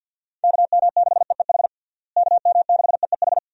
Morse code, heard as a keyed tone, about 47 wpm, 700 Hz. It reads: KM6EEH KM6EEH
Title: USA Call Signs - 6 Characters - Repeated 2x - 50wpm